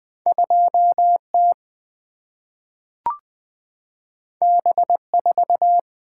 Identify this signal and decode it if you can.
Morse code, heard as a keyed tone, about 20 wpm, 700 Hz. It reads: 2 T B 4
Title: Single-Letter Single-Number 20wpm